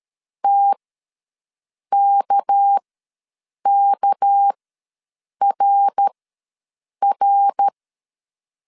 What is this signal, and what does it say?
Morse code, heard as a keyed tone, about 13 wpm, spaced as for 7 wpm, 800 Hz. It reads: TKKRR
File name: LEZIONI / 6 PKX0 / 6 PK ws 12wpm 3116